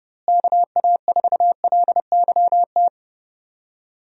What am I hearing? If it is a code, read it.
Morse code, heard as a keyed tone, about 32 wpm, 700 Hz. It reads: KA4LYT
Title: USA Call Signs - 6 Characters - Repeated 3x - 30wpm